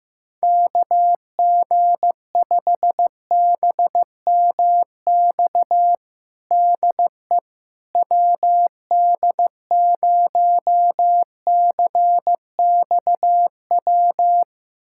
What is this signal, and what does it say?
Morse code, heard as a keyed tone, about 15 wpm, 700 Hz. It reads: KG5BMX DE WD0CXW
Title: USA Callsign DE Callsign - 6 Characters - 15wpm